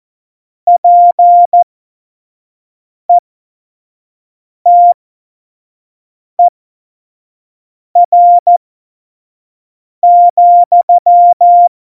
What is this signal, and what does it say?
Morse code, heard as a keyed tone, about 14 wpm, spaced as for 5 wpm, 700 Hz. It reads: PETER,
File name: Morsekurs / Witze / Witz 23. Juli 2019, 25 F70